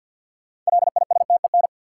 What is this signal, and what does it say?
Morse code, heard as a keyed tone, about 51 wpm, 700 Hz. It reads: LISTEN
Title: Sentences from Top 400 Words - New Word Drill - 50wpm